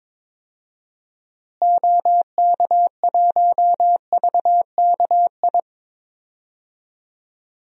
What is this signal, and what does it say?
Morse code, heard as a keyed tone, about 22 wpm, 700 Hz. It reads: OK1VKI
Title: DXCC-100 Call Signs - 6 Characters - Repeated 2x - 22wpm